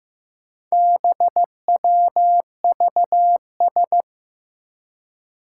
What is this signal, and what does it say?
Morse code, heard as a keyed tone, about 15 wpm, 700 Hz. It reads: BWVS
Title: Four Challenging Letter-Number Combinations 15wpm